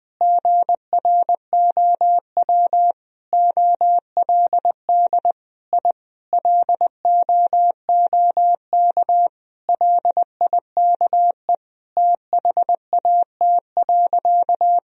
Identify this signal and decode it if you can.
Morse code, heard as a keyed tone, about 20 wpm, 700 Hz. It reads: GROW OLD I LOOK LIKE THAT.
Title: Sentences from Top 1000 Words - Review Part 1 - 20wpm